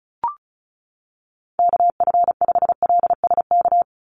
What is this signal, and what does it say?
Morse code, heard as a keyed tone, about 34 wpm, 700 Hz. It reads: KF5LSK
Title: USA Call Signs 35wpm